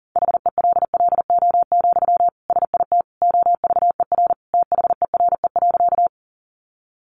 Morse code, heard as a keyed tone, about 40 wpm, 700 Hz. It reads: HELLO, SIT OVER THERE.
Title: Sentences from Top 1100 Words 40wpm